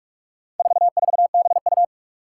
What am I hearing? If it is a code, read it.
Morse code, heard as a keyed tone, about 44 wpm, 700 Hz. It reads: 44BU
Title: Four Challenging Letter-Number Combinations 45wpm